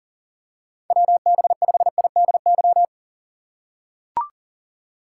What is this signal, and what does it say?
Morse code, heard as a keyed tone, about 40 wpm, 700 Hz. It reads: WB5IDY
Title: K1USN SST QSO Practice - 40wpm